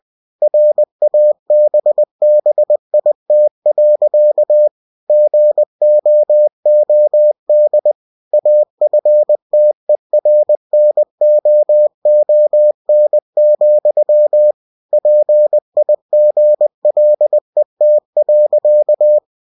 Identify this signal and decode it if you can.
Morse code, heard as a keyed tone, about 20 wpm, 600 Hz. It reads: RABBIT. GOOD AFTERNOON, PIGLET.
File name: cw-books / winnie-the-pooh / winnie-20wpm-page-0066